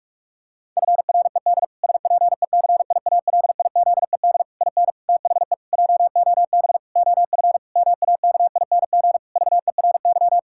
Morse code, heard as a keyed tone, about 41 wpm, 700 Hz. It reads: FRED SPECIALIZED IN THE JOB OF MAKING VERY
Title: Pangram Challenge v2 - 45wpm